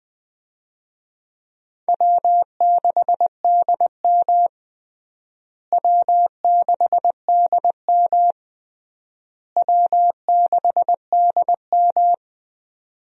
Morse code, heard as a keyed tone, about 20 wpm, 700 Hz. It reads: W6DM W6DM W6DM
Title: USA Call Signs - 4 Characters - Repeated 3x - 20wpm